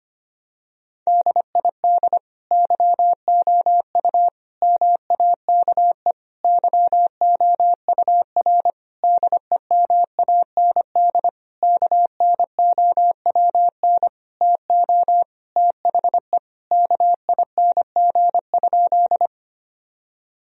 Morse code, heard as a keyed tone, about 25 wpm, 700 Hz. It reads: DID YOU MAKE YOUR DEMAND KNOWN TO THE KING?
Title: Sentences from Top 1000 Words - Review Part 4 - 25wpm